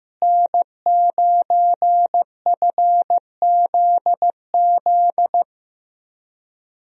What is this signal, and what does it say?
Morse code, heard as a keyed tone, about 15 wpm, 700 Hz. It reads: N9FZZ
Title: USA Call Signs - 5 Characters - Repeated 3x - 15wpm